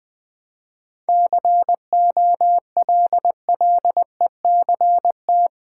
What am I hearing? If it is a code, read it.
Morse code, heard as a keyed tone, about 21 wpm, 700 Hz. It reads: COLLECT
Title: Easy Sentences with Hints - 20wpm